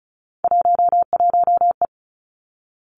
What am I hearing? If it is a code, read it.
Morse code, heard as a keyed tone, about 36 wpm, 700 Hz. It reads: 11E
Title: Three Challenging Letter-Number Combinations 35wpm